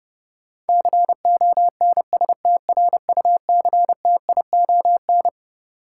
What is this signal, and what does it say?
Morse code, heard as a keyed tone, about 30 wpm, 700 Hz. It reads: CONSTRUCTION
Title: Sentences from Top 1600 Words - New Word Drill - 30wpm